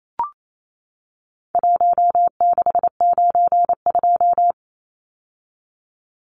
Morse code, heard as a keyed tone, about 28 wpm, 700 Hz. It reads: 1692
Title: Numbers - 4-Digit 28wpm